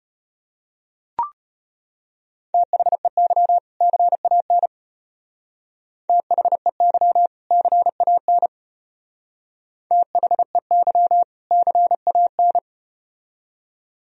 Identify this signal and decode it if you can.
Morse code, heard as a keyed tone, about 33 wpm, 700 Hz. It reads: THEY CAN THEY CAN THEY CAN